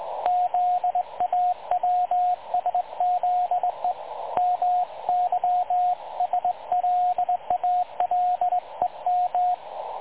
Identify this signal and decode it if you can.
Morse code, heard as a keyed tone, about 19 wpm, 720 Hz, with radio noise. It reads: ZAWSZE MYSLALEM
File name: dlazaawansowanych / nagraniazpasma / websdr recording 3565.0kHz